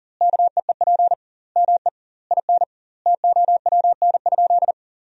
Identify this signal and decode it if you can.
Morse code, heard as a keyed tone, about 37 wpm, 700 Hz. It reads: KEEP ME IN TOWN?
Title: 2x Word-Spacing - Sentences from Top 500 Words - 40wpm